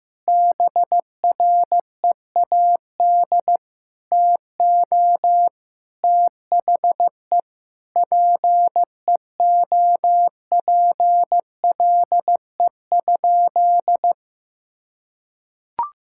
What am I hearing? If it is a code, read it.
Morse code, heard as a keyed tone, about 16 wpm, 700 Hz. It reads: BREAD TO THE PEOPLE?
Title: Sentences from Top 1600 Words 15wpm